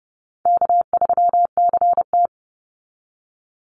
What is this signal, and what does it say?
Morse code, heard as a keyed tone, about 30 wpm, 700 Hz. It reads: K3CT
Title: Call Signs Repeated 3x - 30wpm